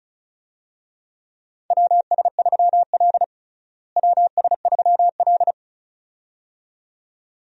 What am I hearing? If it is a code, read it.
Morse code, heard as a keyed tone, about 36 wpm, 700 Hz. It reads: WS3L WS3L
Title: USA Call Signs - 4 Characters - Repeated 2x - 35wpm